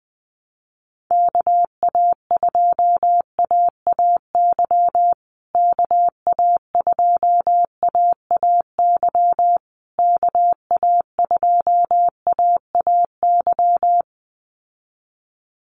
Morse code, heard as a keyed tone, about 20 wpm, 700 Hz. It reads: KA2AAY KA2AAY KA2AAY
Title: USA Call Signs Repeated 3x 20wpm